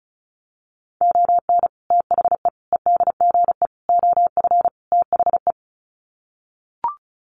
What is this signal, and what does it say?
Morse code, heard as a keyed tone, about 34 wpm, 700 Hz. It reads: ON THE EDGE OF THE
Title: Sets of 5 Words 35wpm